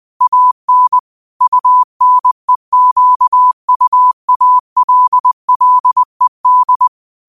Morse code, heard as a keyed tone, about 20 wpm, 1000 Hz. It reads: AN UNEQUALLED
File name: qotdfiles / 20wpm